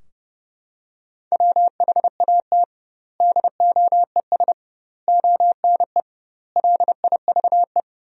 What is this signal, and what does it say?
Morse code, heard as a keyed tone, about 32 wpm, 700 Hz. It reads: WHAT DOES ONE LIVE
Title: 2x Word-Spacing - Sentences from Top 200 Words - 30wpm